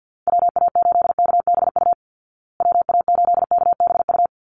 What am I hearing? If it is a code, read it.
Morse code, heard as a keyed tone, about 50 wpm, 700 Hz. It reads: WA8KBU WA8KBU
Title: Call Signs Repeated 3x - 50wpm